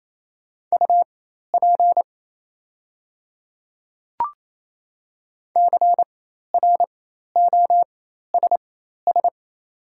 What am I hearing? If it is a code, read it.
Morse code, heard as a keyed tone, about 28 wpm, spaced as for 13 wpm, 690 Hz. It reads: UP CROSS
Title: 4x Character Spacing - Top 500 Words - 28wpm